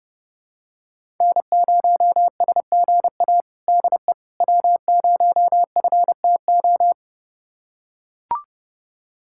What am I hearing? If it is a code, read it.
Morse code, heard as a keyed tone, about 30 wpm, 700 Hz. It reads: N0SGA DE W0FTO
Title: USA Callsign DE Callsign - 5 Characters - 30wpm